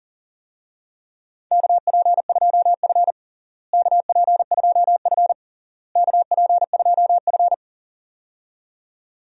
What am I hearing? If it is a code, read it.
Morse code, heard as a keyed tone, about 40 wpm, 700 Hz. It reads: KP2F KP2F KP2F
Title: DXCC-100 Call Signs - 4 Characters - Repeated 3x - 40wpm